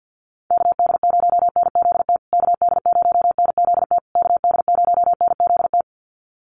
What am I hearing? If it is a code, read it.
Morse code, heard as a keyed tone, about 50 wpm, 700 Hz. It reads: KD0NZT KD0NZT KD0NZT
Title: USA Call Signs Repeated 3x 50wpm